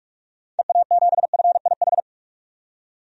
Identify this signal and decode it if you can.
Morse code, heard as a keyed tone, about 45 wpm, 700 Hz. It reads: EA7FIH
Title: DXCC-100 Call Signs - 6 Characters - 45wpm